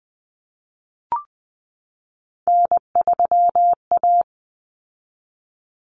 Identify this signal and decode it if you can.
Morse code, heard as a keyed tone, about 20 wpm, 700 Hz. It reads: N3A
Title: Common Cut Numbers - 3-Digit 20wpm